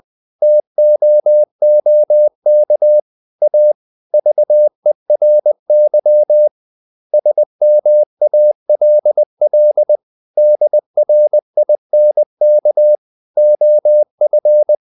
Morse code, heard as a keyed tone, about 20 wpm, 600 Hz. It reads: TOOK A VERY SMALL DRINK OF
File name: the old man and the sea part 2 20wpm